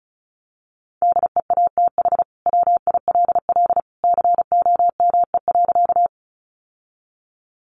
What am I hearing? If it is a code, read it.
Morse code, heard as a keyed tone, about 34 wpm, 700 Hz. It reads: DEATH WILL COME.
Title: Sentences from Top 1100 Words 35wpm